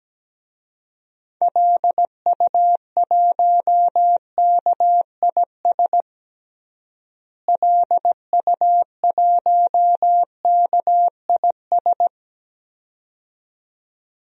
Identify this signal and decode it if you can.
Morse code, heard as a keyed tone, about 17 wpm, 700 Hz. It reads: LU1KIS LU1KIS
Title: DXCC-100 Call Signs - 6 Characters - Repeated 2x - 17wpm